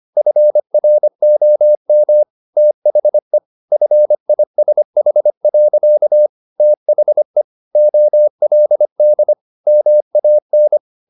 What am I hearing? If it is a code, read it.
Morse code, heard as a keyed tone, about 25 wpm, 600 Hz. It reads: FROM THE FISH. THE OLD MAN